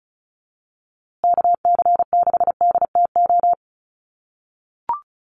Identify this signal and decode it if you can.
Morse code, heard as a keyed tone, about 34 wpm, 700 Hz. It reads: KC6DTO
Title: USA Call Signs Repeated 3x 35wpm